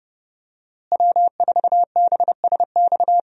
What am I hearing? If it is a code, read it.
Morse code, heard as a keyed tone, about 30 wpm, 700 Hz. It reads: W4BSX
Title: USA Call Signs - 5 Characters - Repeated 2x - 30wpm